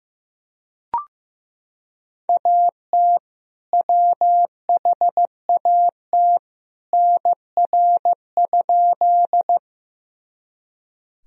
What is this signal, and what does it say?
Morse code, heard as a keyed tone, about 15 wpm, 700 Hz. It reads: AT WHAT NR?
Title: Sentences from Top 100 Words and Abbreviations 15wpm